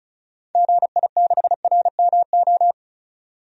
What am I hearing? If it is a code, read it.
Morse code, heard as a keyed tone, about 36 wpm, 700 Hz. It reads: GI6RMO
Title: DXCC-100 Call Signs - 6 Characters - 35wpm